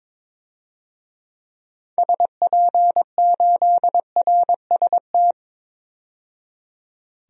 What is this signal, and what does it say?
Morse code, heard as a keyed tone, about 22 wpm, 700 Hz. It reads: SP8RST
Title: DXCC-100 Call Signs - 6 Characters - 22wpm